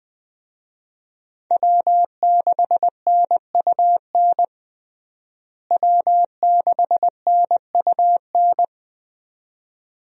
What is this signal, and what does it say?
Morse code, heard as a keyed tone, about 20 wpm, 700 Hz. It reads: W6NUN W6NUN
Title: USA Call Signs - 5 Characters - Repeated 2x - 20wpm